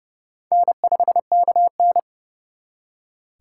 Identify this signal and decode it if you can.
Morse code, heard as a keyed tone, about 36 wpm, 700 Hz. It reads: N5KN
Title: Speed Racing - USA 1x2 Call Signs 30wpm